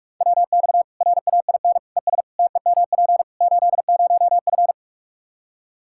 Morse code, heard as a keyed tone, about 43 wpm, 700 Hz. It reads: WX RAIN ES TEMP 80F
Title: Practice QSO - Exchange 2 - Info - 45wpm